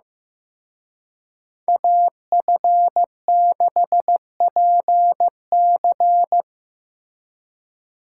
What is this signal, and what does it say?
Morse code, heard as a keyed tone, about 18 wpm, 700 Hz. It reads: AF6PC
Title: Speed Racing - USA 2x2 Call Signs 15wpm